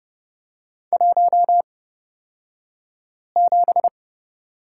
Morse code, heard as a keyed tone, about 30 wpm, spaced as for 5 wpm, 700 Hz. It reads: 17
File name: Lesson 36d - BT - Two Characters 30wpm